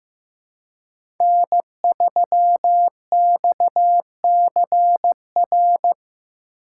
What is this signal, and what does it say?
Morse code, heard as a keyed tone, about 15 wpm, 700 Hz. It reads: N3XCR
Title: USA Call Signs - 5 Characters - 3x Character Spacing - 15wpm